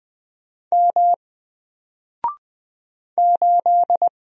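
Morse code, heard as a keyed tone, about 20 wpm, 700 Hz. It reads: M 8
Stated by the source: Single letter and number Morse drill at 20 wpm, from N4LCD.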